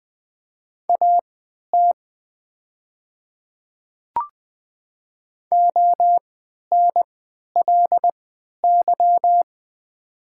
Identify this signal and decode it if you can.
Morse code, heard as a keyed tone, about 20 wpm, spaced as for 11 wpm, 700 Hz. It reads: AT ONLY
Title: 3x Character Spacing - Top 200 Words - 20wpm